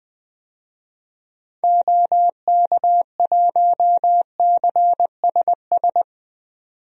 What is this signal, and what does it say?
Morse code, heard as a keyed tone, about 21 wpm, 700 Hz. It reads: OK1CSS
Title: DXCC-100 Call Signs - 6 Characters - 20wpm